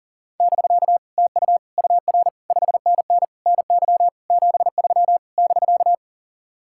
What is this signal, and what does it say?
Morse code, heard as a keyed tone, about 50 wpm, 700 Hz. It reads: <BK> TU UR 5NN NY 73 <BK>
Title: Speed-Racing Common QSO Phrases - 40wpm